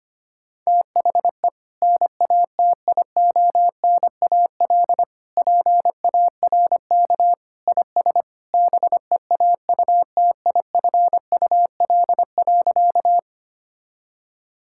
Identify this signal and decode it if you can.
Morse code, heard as a keyed tone, about 25 wpm, 700 Hz. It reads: THE NATIONAL PARK IS BEAUTIFUL.
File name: practice / 25wpm / Sentences with 2-Word Phrases - 25wpm